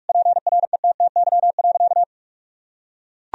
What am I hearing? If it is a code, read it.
Morse code, heard as a keyed tone, about 41 wpm, 700 Hz. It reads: PRETTY.
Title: Sentences from Top 1000 Words - Review Part 1 - 45wpm